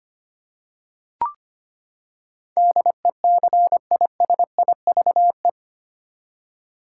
Morse code, heard as a keyed tone, about 25 wpm, 700 Hz. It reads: DECISIVE
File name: Eight Letter Words - 25wpm